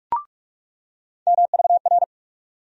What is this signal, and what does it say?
Morse code, heard as a keyed tone, about 46 wpm, 700 Hz. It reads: MVR
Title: USA Call Sign Postfixes - 3 Letters 45wpm